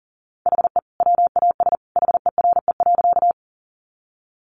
Morse code, heard as a keyed tone, about 40 wpm, 700 Hz. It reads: HE WAS HERE.
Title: Sentences from Top 200 Words 40wpm